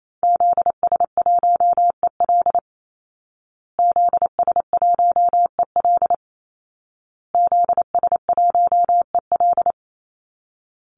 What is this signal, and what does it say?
Morse code, heard as a keyed tone, about 28 wpm, 700 Hz. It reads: ZS1EL ZS1EL ZS1EL
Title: Call Signs Repeated 2x - 28wpm